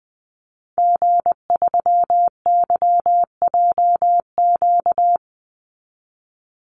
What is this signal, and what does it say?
Morse code, heard as a keyed tone, about 20 wpm, 700 Hz. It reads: G3YJQ
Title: Call Signs Repeated 3x - 20wpm